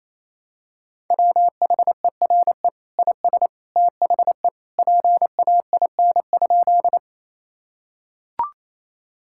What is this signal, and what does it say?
Morse code, heard as a keyed tone, about 28 wpm, 700 Hz. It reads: WHERE IS THE PAIN?
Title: Sentences from Top 1600 Words 28wpm